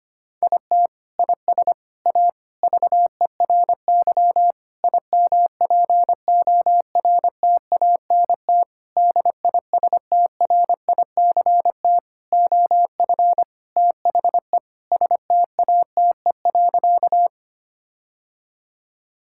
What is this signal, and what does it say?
Morse code, heard as a keyed tone, about 25 wpm, 700 Hz. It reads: IT IS A VERY IMPORTANT DISTRICT OF THE STATE.
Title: Sentences from Top 1600 Words 25wpm